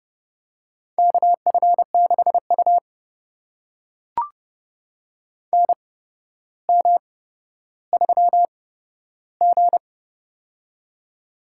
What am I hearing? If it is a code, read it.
Morse code, heard as a keyed tone, about 30 wpm, spaced as for 13 wpm, 700 Hz. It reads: KF6U NM3G
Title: USA Call Signs - 4 Characters - 8x Character Spacing - 30wpm